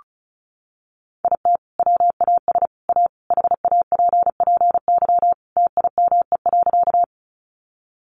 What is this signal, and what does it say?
Morse code, heard as a keyed tone, about 34 wpm, 700 Hz. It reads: IT WAS A HAPPY TIME.
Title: Sentences from Top 1000 Words 35wpm